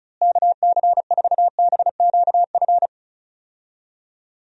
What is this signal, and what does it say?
Morse code, heard as a keyed tone, about 36 wpm, 700 Hz. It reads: KC4BQF
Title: USA Call Signs - 6 Characters - Repeated 3x - 35wpm